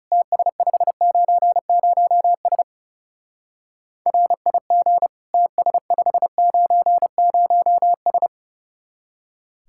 Morse code, heard as a keyed tone, about 36 wpm, 700 Hz. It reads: TS590S RIG TS590S
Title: Speed-Racing Common QSO Phrases Encore - 30wpm